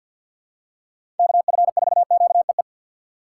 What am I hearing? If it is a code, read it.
Morse code, heard as a keyed tone, about 51 wpm, 700 Hz. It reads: KF4QEE
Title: USA Call Signs - 6 Characters - 4x Character Spacing - 50wpm